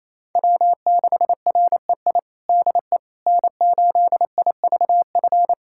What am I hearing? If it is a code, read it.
Morse code, heard as a keyed tone, about 28 wpm, 700 Hz. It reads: W6REI DE N8IVF
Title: USA Callsign DE Callsign 28wpm